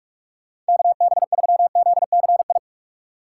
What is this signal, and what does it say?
Morse code, heard as a keyed tone, about 57 wpm, 700 Hz. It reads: KB3ZCI